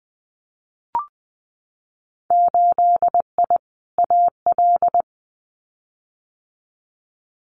Morse code, heard as a keyed tone, about 19 wpm, 700 Hz. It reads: 8I AL
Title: Winter Field Day Exchanges Encore 20wpm